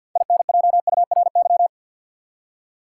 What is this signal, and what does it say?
Morse code, heard as a keyed tone, about 47 wpm, 700 Hz. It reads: INJURY
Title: Six Letter Words - 50wpm